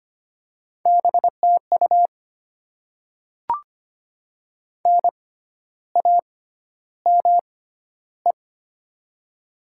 Morse code, heard as a keyed tone, about 25 wpm, spaced as for 12 wpm, 700 Hz. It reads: BTU NAME
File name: Top 20 QSO Elements - 6x Character Spacing - 25wpm